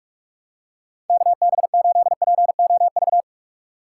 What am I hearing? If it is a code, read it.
Morse code, heard as a keyed tone, about 41 wpm, 700 Hz. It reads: KB8POV
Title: USA Call Signs - 6 Characters - 6x Character Spacing - 45wpm